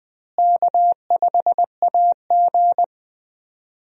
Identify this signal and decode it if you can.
Morse code, heard as a keyed tone, about 20 wpm, 700 Hz. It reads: K5AG
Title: USA Call Signs - 4 Characters - Repeated 2x - 20wpm